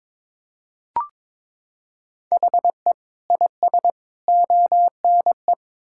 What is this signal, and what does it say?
Morse code, heard as a keyed tone, about 22 wpm, 700 Hz. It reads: HE IS ONE
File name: practice / 22wpm / Sentences with 3-Word Phrases - 22wpm